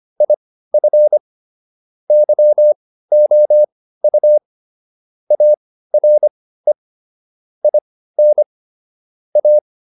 Morse code, heard as a keyed tone, about 25 wpm, spaced as for 15 wpm, 600 Hz. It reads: IF YOU ARE IN A